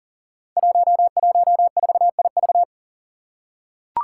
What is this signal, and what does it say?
Morse code, heard as a keyed tone, about 38 wpm, 700 Hz. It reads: 114IV
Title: Five Challenging Letter-Number Combinations 40wpm